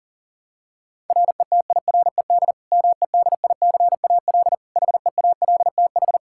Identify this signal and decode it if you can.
Morse code, heard as a keyed tone, about 40 wpm, 700 Hz. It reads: RETIRED MEDICAL HEALTH
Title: Practice QSO - Exchange 3 - More Info - 40wpm